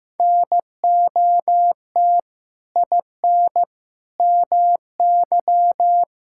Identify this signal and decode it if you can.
Morse code, heard as a keyed tone, about 15 wpm, 700 Hz. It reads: NOT IN MY
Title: Sentences from Top 1000 Words - Review Part 5 - 15wpm